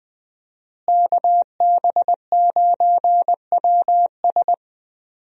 Morse code, heard as a keyed tone, about 20 wpm, 700 Hz. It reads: KB9WS
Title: USA Call Signs - 5 Characters - Repeated 2x - 20wpm